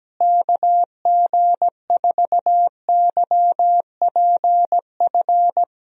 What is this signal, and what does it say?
Morse code, heard as a keyed tone, about 17 wpm, 700 Hz. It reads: KG4YPF
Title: USA Call Signs - 6 Characters - Repeated 3x - 17wpm